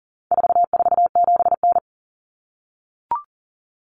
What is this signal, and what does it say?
Morse code, heard as a keyed tone, about 40 wpm, 700 Hz. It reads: 447N
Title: Common Cut Numbers - 4-Digit 40wpm